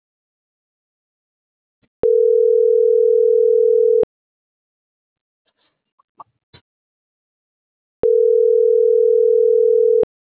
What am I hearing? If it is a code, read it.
Ringback tone (North America), 440 + 480 Hz.